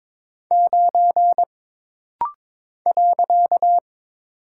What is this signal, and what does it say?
Morse code, heard as a keyed tone, about 23 wpm, 700 Hz. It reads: 9 .
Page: This rapid-fire timing is geared to push your ICR (Instant Character Recognition) skills.